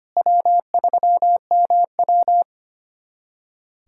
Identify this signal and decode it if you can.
Morse code, heard as a keyed tone, about 25 wpm, 700 Hz. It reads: W3MW
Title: USA Callsign DE Callsign - 4 Characters - 25wpm